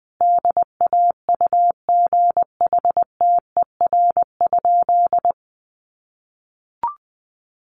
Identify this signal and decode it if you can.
Morse code, heard as a keyed tone, about 20 wpm, 700 Hz. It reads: DAUGHTER?
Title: Sentences from Top 1500 Words 20wpm